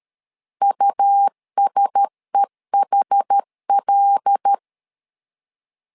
Morse code, heard as a keyed tone, about 13 wpm, 800 Hz. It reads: USEHL
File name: LEZIONI / 5 LBVC / 5 L 12wpm 1913